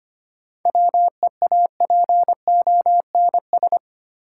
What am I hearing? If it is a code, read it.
Morse code, heard as a keyed tone, about 25 wpm, 700 Hz. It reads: WEAPONS
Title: Sets of 5 Words Encore - New Word Drill - 25wpm